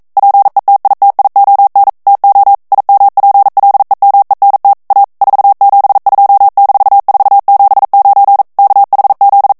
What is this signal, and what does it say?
Morse code, heard as a keyed tone, about 42 wpm, 800 Hz. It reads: PETITION TO IMPLEMENT A 472-479 KHZ